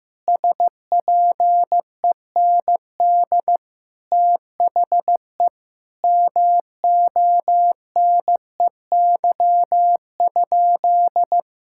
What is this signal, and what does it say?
Morse code, heard as a keyed tone, about 15 wpm, 700 Hz. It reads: SPEND THE MONEY?
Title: Sentences from Top 1000 Words - Review Part 5 - 15wpm